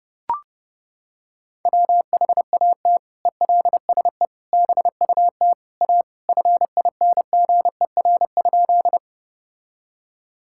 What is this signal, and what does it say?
Morse code, heard as a keyed tone, about 30 wpm, 700 Hz. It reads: WHAT ELSE BUT A FINGER?
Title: Sentences from Top 1600 Words 30wpm